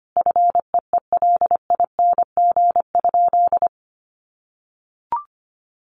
Morse code, heard as a keyed tone, about 25 wpm, 700 Hz. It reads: FEELING?
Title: Sentences from Top 1200 Words 25wpm